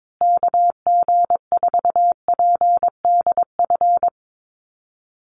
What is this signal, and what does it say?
Morse code, heard as a keyed tone, about 22 wpm, 690 Hz. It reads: KG4PDF
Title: USA Call Signs Repeated 3x 22wpm